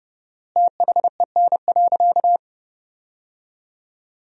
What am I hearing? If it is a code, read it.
Morse code, heard as a keyed tone, about 30 wpm, 700 Hz. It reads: THEN.
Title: 4x Word-Spacing - Sentences from Top 300 Words - 30wpm